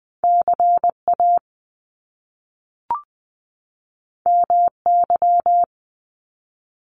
Morse code, heard as a keyed tone, about 20 wpm, 700 Hz. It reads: CA MY
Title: Top Two Letter Combinations in Words 20wpm